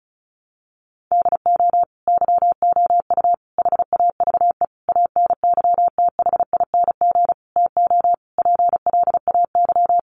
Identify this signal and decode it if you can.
Morse code, heard as a keyed tone, about 34 wpm, 700 Hz. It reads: DO YOU HAVE ANYTHING TO PLAY
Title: Sentences from Top 1000 Words 35wpm